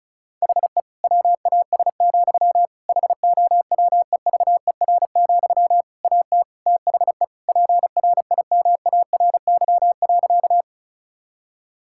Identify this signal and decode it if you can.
Morse code, heard as a keyed tone, about 34 wpm, 700 Hz. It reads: HE WAS, HOWEVER, AT THE PRIMARY.
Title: Sentences from Top 1000 Words - Review Part 5 - 35wpm